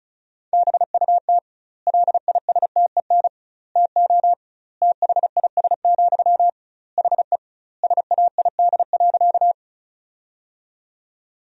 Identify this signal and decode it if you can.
Morse code, heard as a keyed tone, about 36 wpm, 700 Hz. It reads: BUT LISTEN TO THIS, HE SAID.
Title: 2x Word-Spacing - Sentences from Top 400 Words - 35wpm